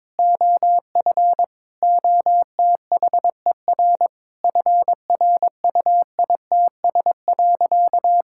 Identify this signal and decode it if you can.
Morse code, heard as a keyed tone, about 22 wpm, 700 Hz. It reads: OF OTHER FRUITS.
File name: practice / 22wpm / Sentences with 7-Word Phrases - 22wpm